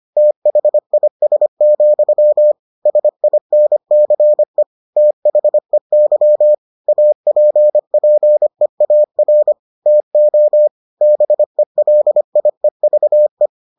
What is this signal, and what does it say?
Morse code, heard as a keyed tone, about 25 wpm, 600 Hz. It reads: THIS, SINCE THEY APPEAR TO BELIEVE